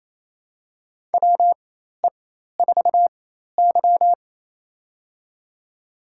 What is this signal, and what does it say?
Morse code, heard as a keyed tone, about 27 wpm, spaced as for 13 wpm, 700 Hz. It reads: WE4Y
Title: USA Call Signs - 4x Character Spacing - 28wpm